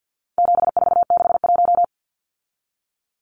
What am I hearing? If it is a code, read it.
Morse code, heard as a keyed tone, about 50 wpm, 700 Hz. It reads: 7461
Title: Numbers - 4-Digit 50wpm